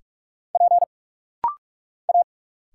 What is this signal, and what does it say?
Morse code, heard as a keyed tone, about 46 wpm, 700 Hz. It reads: P A
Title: Single Letters - Rapid-Fire 45wpm